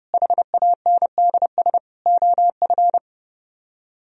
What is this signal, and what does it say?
Morse code, heard as a keyed tone, about 38 wpm, 700 Hz. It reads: HANDS OF